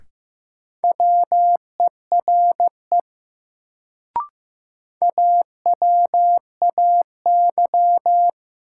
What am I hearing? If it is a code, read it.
Morse code, heard as a keyed tone, about 15 wpm, 700 Hz. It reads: WERE AWAY
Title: Sets of 2 Words - New Word Drill - 15wpm